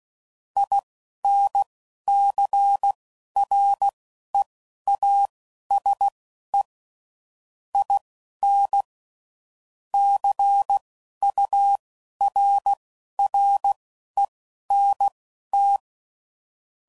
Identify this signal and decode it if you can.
Morse code, heard as a keyed tone, about 16 wpm, spaced as for 11 wpm, 800 Hz. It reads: INCREASE IN CURRENT
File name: CW Code Groups @10WPM / Technical Subjects / XL